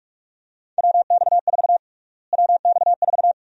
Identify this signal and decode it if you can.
Morse code, heard as a keyed tone, about 47 wpm, 700 Hz. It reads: WX4 WX4
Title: USA Call Sign Prefixes - Repeated 2x - 45wpm